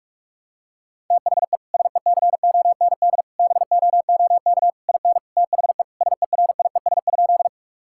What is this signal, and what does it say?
Morse code, heard as a keyed tone, about 46 wpm, 700 Hz. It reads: THE SECOND BOOK IN THE SERIES?
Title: Sentences from Top 1000 Words - Review Part 5 - 45wpm